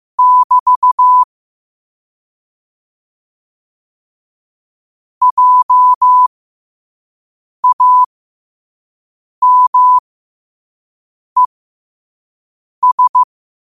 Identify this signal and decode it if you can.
Morse code, heard as a keyed tone, about 15 wpm, spaced as for 5 wpm, 1000 Hz. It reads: <BT> JAMES
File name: qotdfiles / 5wpm